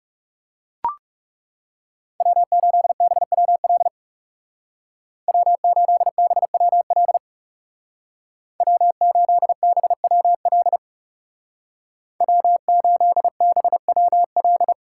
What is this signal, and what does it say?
Morse code, heard as a keyed tone, about 39 wpm, 700 Hz. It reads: W8BWL W8BWL W8BWL W8BWL